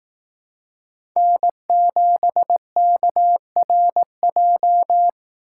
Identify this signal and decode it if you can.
Morse code, heard as a keyed tone, about 18 wpm, 700 Hz. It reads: N7KRJ
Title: Speed Racing - USA 1x3 Call Signs 15wpm